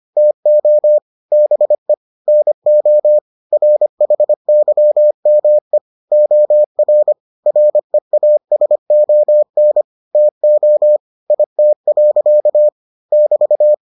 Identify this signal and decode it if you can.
Morse code, heard as a keyed tone, about 25 wpm, 600 Hz. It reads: TO BE NO RHYME OR REASON TO IT. <BT>